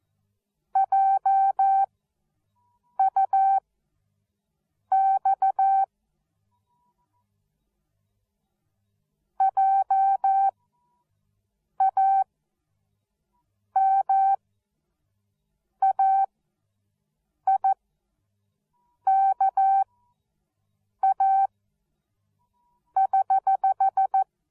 Morse code, heard as a keyed tone, about 14 wpm, spaced as for 5 wpm, 780 Hz. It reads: JUX JAMAIKA<HH>